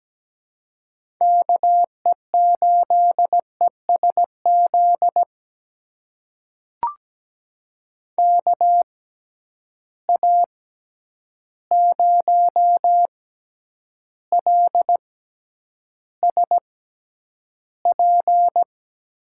Morse code, heard as a keyed tone, about 17 wpm, spaced as for 9 wpm, 700 Hz. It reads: KE8ESZ KA0LSP
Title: USA Call Signs - 6x Character Spacing - 17wpm